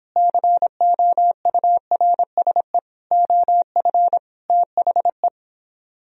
Morse code, heard as a keyed tone, about 26 wpm, 700 Hz. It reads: COURSE OF THE